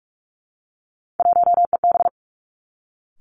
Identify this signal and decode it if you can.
Morse code, heard as a keyed tone, about 46 wpm, 700 Hz. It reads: 1EB
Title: Three Challenging Letter-Number Combinations 45wpm